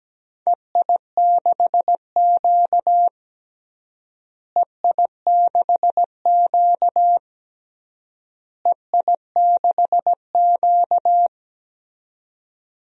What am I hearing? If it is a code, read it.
Morse code, heard as a keyed tone, about 17 wpm, 700 Hz. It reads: EI6Q EI6Q EI6Q
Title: DXCC-100 Call Signs - 4 Characters - Repeated 3x - 17wpm